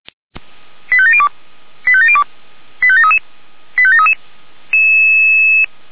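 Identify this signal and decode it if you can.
Five-tone selective call (ZVEI), heard as quick runs of five tones, tones of about 70 ms. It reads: 85702 85702 85733 85733
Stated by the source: ZVEI five-tone selective call received by SDR, address 89247, from a server of Feuerwehr Orlamünde (fire brigade).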